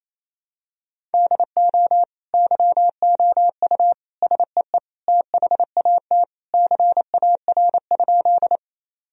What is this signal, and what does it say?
Morse code, heard as a keyed tone, about 28 wpm, 700 Hz. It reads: DO YOU SEE THAT CAR?
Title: Sentences from Top 1000 Words - Review Part 1 - 28wpm